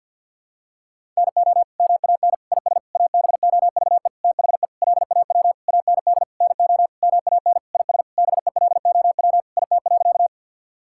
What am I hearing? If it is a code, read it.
Morse code, heard as a keyed tone, about 52 wpm, 700 Hz. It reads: NO MAN IS ABOVE THE LAW AND NO MAN IS BELOW IT.
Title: Sentences from Top 1000 Words - Review Part 5 - 50wpm